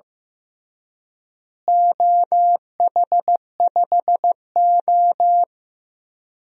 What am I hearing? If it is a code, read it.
Morse code, heard as a keyed tone, about 15 wpm, 700 Hz. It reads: OH5O
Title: DXCC-100 Call Signs - 4 Characters - Repeated 3x - 15wpm